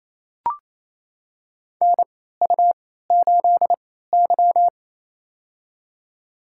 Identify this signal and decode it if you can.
Morse code, heard as a keyed tone, about 28 wpm, spaced as for 16 wpm, 700 Hz. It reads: NU8Y
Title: USA Call Signs - 3x Character Spacing - 28wpm